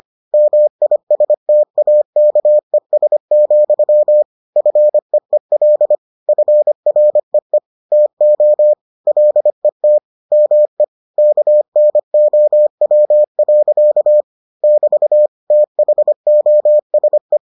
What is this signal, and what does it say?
Morse code, heard as a keyed tone, about 25 wpm, 600 Hz. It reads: MISTAKES, FEEL FREE TO LET ME KNOW. <BT> THOSE